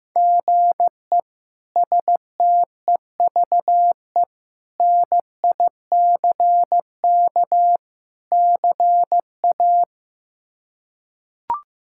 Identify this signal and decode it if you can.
Morse code, heard as a keyed tone, about 15 wpm, 700 Hz. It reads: GE STEVE NICK CA
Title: K1USN SST QSO Practice - 15wpm